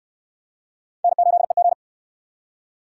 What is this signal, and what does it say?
Morse code, heard as a keyed tone, about 65 wpm, 700 Hz. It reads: N8EG